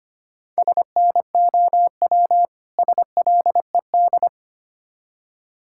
Morse code, heard as a keyed tone, about 26 wpm, 700 Hz. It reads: SNOW SLED
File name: practice / 25wpm / Difficult Words - Speed-Racing with Hints - 25wpm